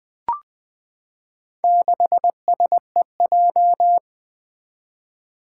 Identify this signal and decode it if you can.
Morse code, heard as a keyed tone, about 20 wpm, 700 Hz. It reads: 6SEJ
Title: Four Challenging Letter-Number Combinations 20wpm